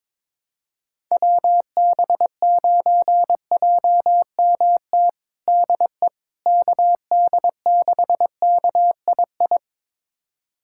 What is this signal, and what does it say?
Morse code, heard as a keyed tone, about 22 wpm, 700 Hz. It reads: WB9JMT DE KD6KII
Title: USA Callsign DE Callsign - 6 Characters - 22wpm